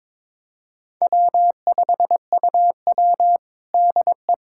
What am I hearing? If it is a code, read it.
Morse code, heard as a keyed tone, about 22 wpm, 700 Hz. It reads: W5UW DE
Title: USA Callsign DE Callsign - 4 Characters - 22wpm